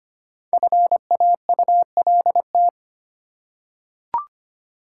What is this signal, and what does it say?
Morse code, heard as a keyed tone, about 25 wpm, 700 Hz. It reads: FAULT
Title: Five Letter Words - 25wpm